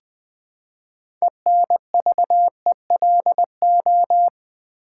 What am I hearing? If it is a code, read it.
Morse code, heard as a keyed tone, about 20 wpm, 700 Hz. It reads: ENVELO
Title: Build A Word - 3 Letters At A Time - 8-Letter Words - 20wpm